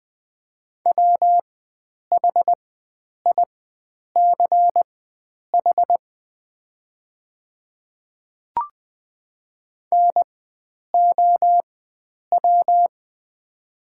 Morse code, heard as a keyed tone, about 20 wpm, spaced as for 9 wpm, 700 Hz. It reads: WHICH NOW
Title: 4x Character Spacing - Top 500 Words - 20wpm